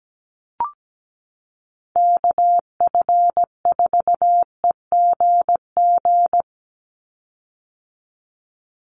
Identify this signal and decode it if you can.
Morse code, heard as a keyed tone, about 17 wpm, 700 Hz. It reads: KF4EGG
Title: USA Call Signs 17wpm